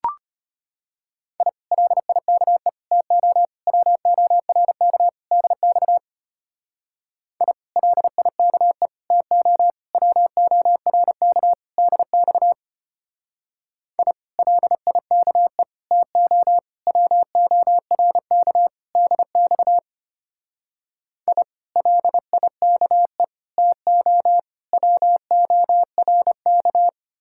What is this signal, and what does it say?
Morse code, heard as a keyed tone, about 31 wpm, 700 Hz. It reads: I LIKE TO WORK DX I LIKE TO WORK DX I LIKE TO WORK DX I LIKE TO WORK
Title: Speed-Racing Common QSO Phrases Encore - 25wpm